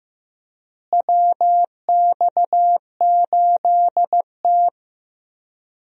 Morse code, heard as a keyed tone, about 15 wpm, 700 Hz. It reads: WX8T
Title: USA Call Signs - 4 Characters - Repeated 3x - 15wpm